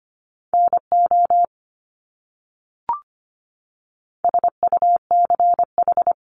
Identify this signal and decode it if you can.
Morse code, heard as a keyed tone, about 25 wpm, 700 Hz. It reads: NO SUCH
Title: Top 200 Words Repeated 25wpm